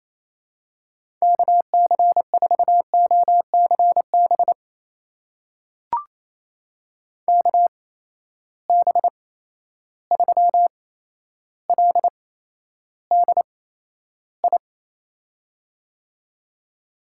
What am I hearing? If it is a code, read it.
Morse code, heard as a keyed tone, about 28 wpm, spaced as for 12 wpm, 700 Hz. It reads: KC4OCB KB3LDI
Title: USA Call Signs - 6 Characters - 8x Character Spacing - 28wpm